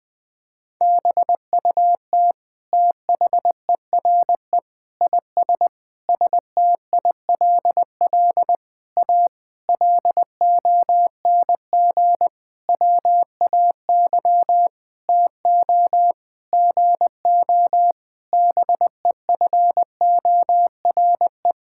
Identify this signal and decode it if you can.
Morse code, heard as a keyed tone, about 21 wpm, 700 Hz. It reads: BUT THERE IS STILL A LONG WAY TO GO BEFORE
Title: Sentences with 9-Word Phrases - Spoken Beginning - 20wpm